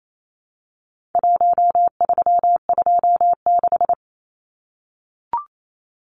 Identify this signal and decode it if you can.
Morse code, heard as a keyed tone, about 28 wpm, 700 Hz. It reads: 1326
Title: Numbers - 4-Digit 28wpm